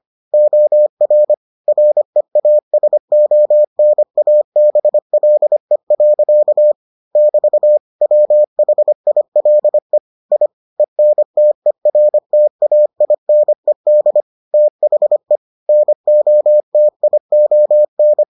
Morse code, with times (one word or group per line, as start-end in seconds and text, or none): 0.3-1.3 OR
1.7-6.7 REASONABLE.
7.2-7.8 <BT>
8.0-10.0 WHILE
10.3-10.5 I
10.8-14.2 ENTERTAINED
14.5-15.4 THE
15.7-18.2 NOTION